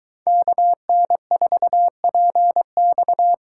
Morse code, heard as a keyed tone, about 22 wpm, 700 Hz. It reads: KN4PX